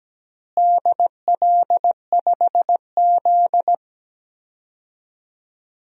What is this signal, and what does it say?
Morse code, heard as a keyed tone, about 17 wpm, 700 Hz. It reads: DL5Z
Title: DXCC-100 Call Signs - 4 Characters - Repeated 2x - 17wpm